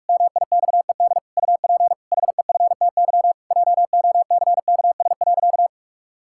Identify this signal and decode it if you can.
Morse code, heard as a keyed tone, about 41 wpm, 700 Hz. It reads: MIXED UP HEFTY JOCKS.
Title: Pangram Challenge v2 - 45wpm